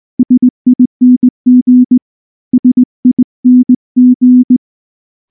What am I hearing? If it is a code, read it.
Morse code, heard as a keyed tone, about 19 wpm, 250 Hz. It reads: SING SING
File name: morse / morse-training-20WPM-8